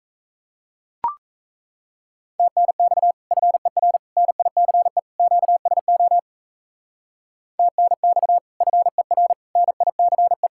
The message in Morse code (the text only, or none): TNX FER NICE QSO TNX FER NICE